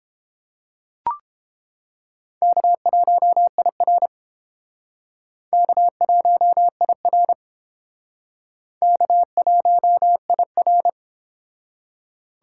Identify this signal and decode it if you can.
Morse code, heard as a keyed tone, about 31 wpm, 700 Hz. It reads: K1IR K1IR K1IR